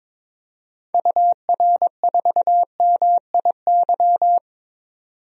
Morse code, heard as a keyed tone, about 21 wpm, 700 Hz. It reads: UR4MIY
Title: DXCC-100 Call Signs - 6 Characters - 22wpm